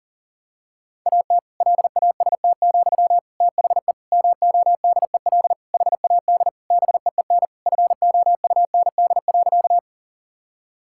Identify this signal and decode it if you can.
Morse code, heard as a keyed tone, about 40 wpm, 700 Hz. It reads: AT LAST, THE MODEL HAD BEEN FOUND.
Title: Sentences from Top 1000 Words - Review Part 5 - 40wpm